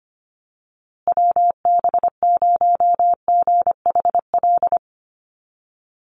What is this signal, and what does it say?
Morse code, heard as a keyed tone, about 25 wpm, 700 Hz. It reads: WB0GHL
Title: USA Call Signs Repeated 2x 25wpm